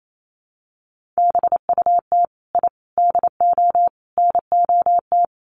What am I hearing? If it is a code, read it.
Morse code, heard as a keyed tone, about 28 wpm, 700 Hz. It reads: BUT I DO NOT
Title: Sets of 4 Words 28wpm